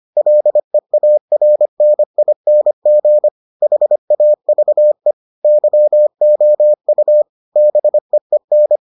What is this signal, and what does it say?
Morse code, heard as a keyed tone, about 25 wpm, 600 Hz. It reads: LEARNING HAVE YOU BEEN